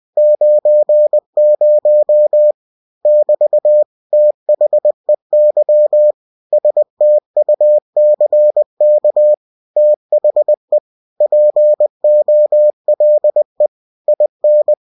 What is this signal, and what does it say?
Morse code, heard as a keyed tone, about 20 wpm, 600 Hz. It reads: 90 <BT> THEY STUCK THE POLE IN